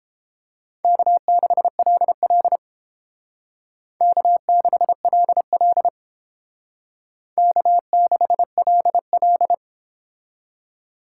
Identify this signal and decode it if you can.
Morse code, heard as a keyed tone, about 29 wpm, 700 Hz. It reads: K6LL K6LL K6LL